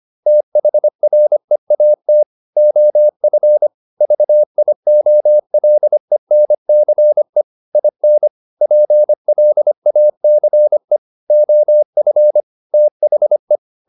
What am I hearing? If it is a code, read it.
Morse code, heard as a keyed tone, about 25 wpm, 600 Hz. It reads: THREAT OF VIOLENCE IN PLACE OF THE
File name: Never 0025wpm